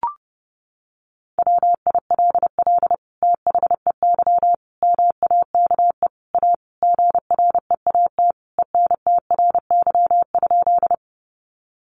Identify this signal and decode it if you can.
Morse code, heard as a keyed tone, about 30 wpm, 700 Hz. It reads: WILL THEY MAKE A GREAT ENTRY?
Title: Sentences from Top 1500 Words 30wpm